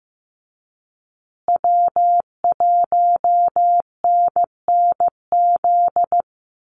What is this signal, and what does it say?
Morse code, heard as a keyed tone, about 15 wpm, 700 Hz. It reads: W1NNZ
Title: USA Call Signs Repeated 3x 15wpm